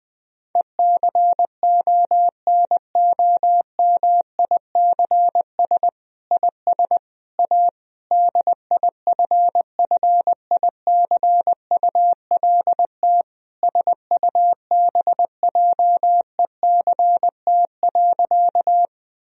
Morse code, heard as a keyed tone, about 20 wpm, 700 Hz. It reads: ECONOMICS IS A DIFFICULT SUBJECT.
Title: Sentences from Top 1000 Words - Review Part 3 - 20wpm